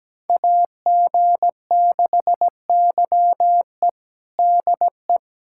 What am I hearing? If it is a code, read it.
Morse code, heard as a keyed tone, about 17 wpm, 700 Hz. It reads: AG6YE DE
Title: USA Callsign DE Callsign 17wpm